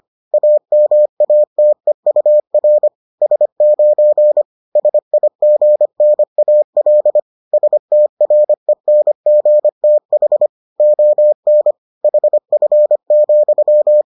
Morse code, heard as a keyed tone, about 25 wpm, 600 Hz. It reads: AMATEUR S9 SIGNAL STRENGTH ON HF,